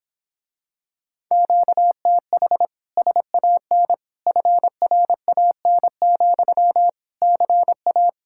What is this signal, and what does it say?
Morse code, heard as a keyed tone, about 26 wpm, 700 Hz. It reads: QTH SAN FRAN, CA